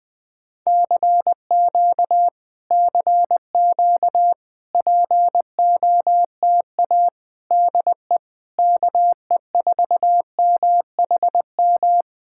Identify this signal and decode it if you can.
Morse code, heard as a keyed tone, about 19 wpm, 700 Hz. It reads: CQ CQ POTA DE KE4MHM
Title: Speed-Racing Common QSO Phrases Encore - 20wpm